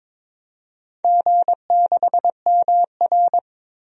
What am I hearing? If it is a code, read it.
Morse code, heard as a keyed tone, about 23 wpm, 700 Hz. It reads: G6MR
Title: DXCC-100 Call Signs - 4 Characters - 22wpm